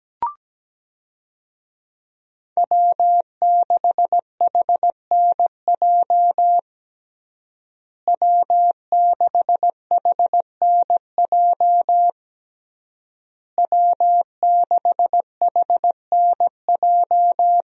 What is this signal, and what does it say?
Morse code, heard as a keyed tone, about 17 wpm, 700 Hz. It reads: W6HNJ W6HNJ W6HNJ
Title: USA Call Signs - 5 Characters - Repeated 3x - 17wpm